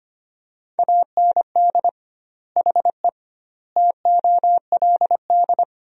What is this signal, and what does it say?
Morse code, heard as a keyed tone, about 25 wpm, 700 Hz. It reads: AND HE TOLD
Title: 2x Word-Spacing - Sentences from Top 500 Words - 25wpm